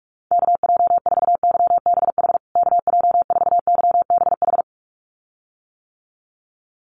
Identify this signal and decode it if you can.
Morse code, heard as a keyed tone, about 44 wpm, 700 Hz. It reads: KJ4YBH KJ4YBH
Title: USA Call Signs Repeated 2x 45wpm